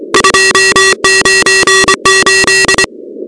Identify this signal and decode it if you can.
Morse code, heard as a keyed tone, about 24 wpm, 1900 Hz, with noise. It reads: 298
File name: contests / 2019-cw-wpx-kt5j-k5pi- -2330 0-13